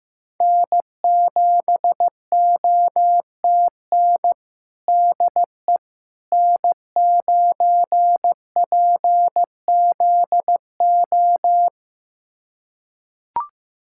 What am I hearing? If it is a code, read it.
Morse code, heard as a keyed tone, about 15 wpm, 700 Hz. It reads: N7OTN DE N9PZO
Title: USA Callsign DE Callsign - 5 Characters - 15wpm